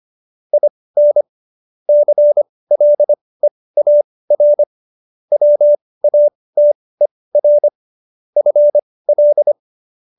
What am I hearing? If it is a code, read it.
Morse code, heard as a keyed tone, about 25 wpm, spaced as for 18 wpm, 590 Hz. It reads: IN CLEAR WATER FL
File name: qso222 18